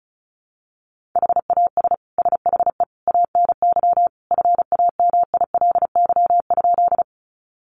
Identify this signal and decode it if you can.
Morse code, heard as a keyed tone, about 34 wpm, 700 Hz. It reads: HAS SHE ANY FAMILY?
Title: Sentences from Top 400 Words 35wpm